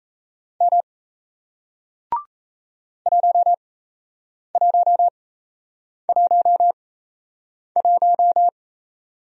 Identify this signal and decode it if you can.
Morse code, heard as a keyed tone, about 37 wpm, spaced as for 8 wpm, 700 Hz. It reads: M 1111